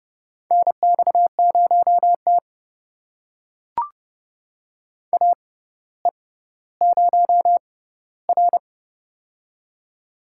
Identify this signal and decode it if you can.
Morse code, heard as a keyed tone, about 30 wpm, spaced as for 15 wpm, 700 Hz. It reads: NX0T AE0R
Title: USA Call Signs - 4 Characters - 6x Character Spacing - 30wpm